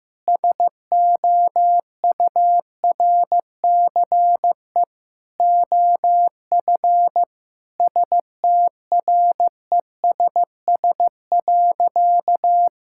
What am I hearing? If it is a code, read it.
Morse code, heard as a keyed tone, about 15 wpm, 700 Hz. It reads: SOURCE OF STRESS.
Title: Sentences from Top 1000 Words - Review Part 6 - 15wpm